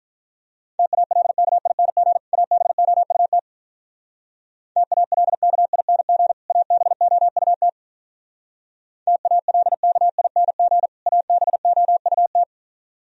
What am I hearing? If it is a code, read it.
Morse code, heard as a keyed tone, about 42 wpm, 700 Hz. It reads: TALKING ABOUT TALKING ABOUT TALKING ABOUT
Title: Speed-Racing - Sets of 2 Words 35wpm